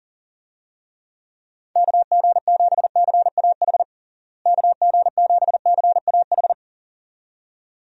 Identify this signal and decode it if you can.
Morse code, heard as a keyed tone, about 40 wpm, 700 Hz. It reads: KG7CAH KG7CAH
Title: USA Call Signs - 6 Characters - Repeated 2x - 40wpm